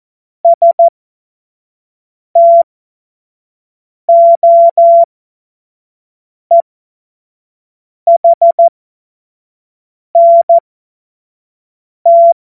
Morse code, heard as a keyed tone, about 14 wpm, spaced as for 5 wpm, 680 Hz. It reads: STOEHNT
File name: Morsekurs / Witze / Witz 23. August 2022 25 F70